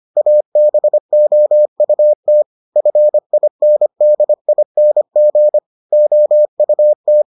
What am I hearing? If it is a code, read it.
Morse code, heard as a keyed tone, about 25 wpm, 600 Hz. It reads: ABOUT FINDING OUT